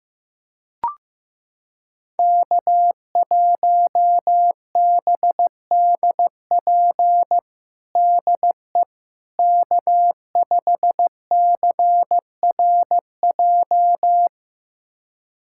Morse code, heard as a keyed tone, about 15 wpm, 700 Hz. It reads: K1BDP DE K5CRJ
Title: USA Callsign DE Callsign - 5 Characters - 15wpm